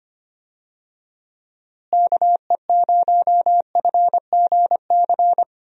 Morse code, heard as a keyed tone, about 25 wpm, 700 Hz. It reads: KE0FGC
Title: USA Call Signs - 6 Characters - Repeated 3x - 25wpm